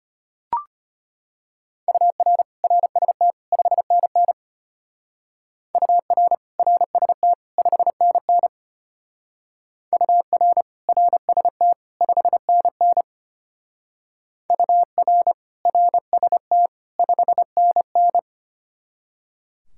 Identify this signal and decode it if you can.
Morse code, heard as a keyed tone, about 31 wpm, 700 Hz. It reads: UR RST 5NN UR RST 5NN UR RST 5NN UR RST 5NN
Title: Speed-Racing Common QSO Phrases - 25wpm